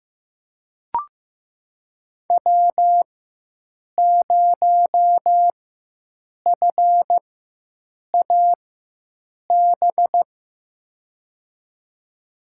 Morse code, heard as a keyed tone, about 15 wpm, spaced as for 7 wpm, 700 Hz. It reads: W0FAB
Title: USA Call Signs - 5 Characters - 4x Character Spacing - 15wpm